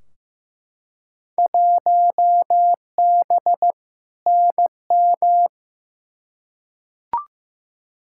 Morse code, heard as a keyed tone, about 15 wpm, 700 Hz. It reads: 1B NM
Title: Field Day Exchanges Encore - 15wpm